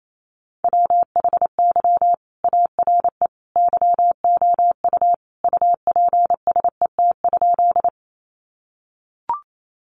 Morse code, heard as a keyed tone, about 28 wpm, 700 Hz. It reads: WHY ARE YOU UPSET?
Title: Sentences from Top 800 Words 28wpm